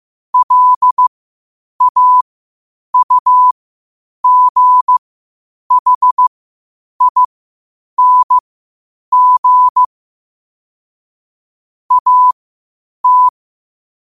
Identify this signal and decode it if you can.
Morse code, heard as a keyed tone, about 15 wpm, spaced as for 8 wpm, 1000 Hz. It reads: LAUGHING AT
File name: qotdfiles / 8wpm